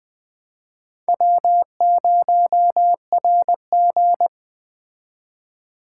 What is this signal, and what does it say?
Morse code, heard as a keyed tone, about 20 wpm, 700 Hz. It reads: W0RG
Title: Speed Racing - USA 1x2 Call Signs 17wpm